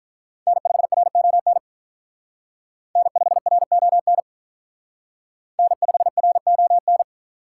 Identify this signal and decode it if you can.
Morse code, heard as a keyed tone, about 45 wpm, 700 Hz. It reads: N5RON N5RON N5RON